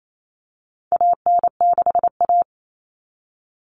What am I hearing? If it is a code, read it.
Morse code, heard as a keyed tone, about 28 wpm, 700 Hz. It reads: AN6A
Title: All Cut Numbers - 4-Digit 28wpm